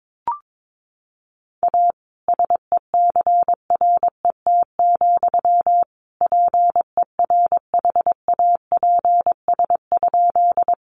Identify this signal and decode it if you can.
Morse code, heard as a keyed tone, about 22 wpm, 700 Hz. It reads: A SECRET, PERHAPS?
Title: Sentences from Top 1200 Words 22wpm